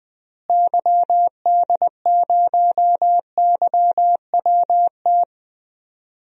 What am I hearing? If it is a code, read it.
Morse code, heard as a keyed tone, about 21 wpm, 700 Hz. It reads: YD0YWT
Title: DXCC-100 Call Signs - 6 Characters - Repeated 3x - 20wpm